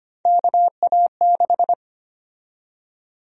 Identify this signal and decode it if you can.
Morse code, heard as a keyed tone, about 25 wpm, 700 Hz. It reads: KA6
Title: USA Call Sign Prefixes - 25wpm